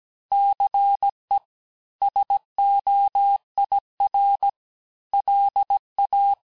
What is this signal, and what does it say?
Morse code, heard as a keyed tone, about 17 wpm, 800 Hz. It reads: CE SOIR LA
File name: corrige-cours-1581-fin-du-QSO-vitesse-16-mpm